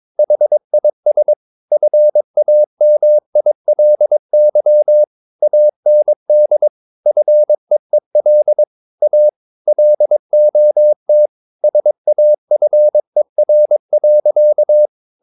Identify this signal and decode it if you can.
Morse code, heard as a keyed tone, about 22 wpm, 600 Hz. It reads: HIS FAMILY AND FEEL A LOT SAFER.
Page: Romanen Never af Ken Follet på engelsk.